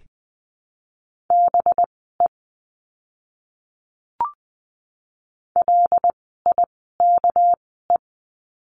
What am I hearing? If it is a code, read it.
Morse code, heard as a keyed tone, about 20 wpm, spaced as for 15 wpm, 700 Hz. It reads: BE LIKE
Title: 244 Lesson 39f - Letter z - Words 20wpm